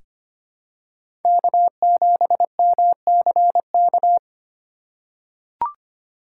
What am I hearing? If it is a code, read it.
Morse code, heard as a keyed tone, about 25 wpm, 700 Hz. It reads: K7MCK
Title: USA Call Signs - 5 Characters - 2x Character Spacing - 25wpm